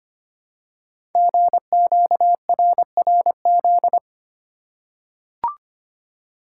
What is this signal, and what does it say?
Morse code, heard as a keyed tone, about 25 wpm, 690 Hz. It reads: GQRRZ
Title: Code Groups - 5 Letters - 25wpm